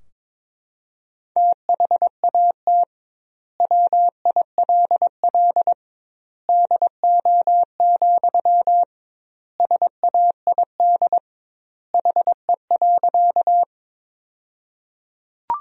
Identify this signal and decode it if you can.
Morse code, heard as a keyed tone, about 22 wpm, 700 Hz. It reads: THAT WILL DO, SAID HE.
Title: 2x Word-Spacing - Sentences from Top 100 Words - 22wpm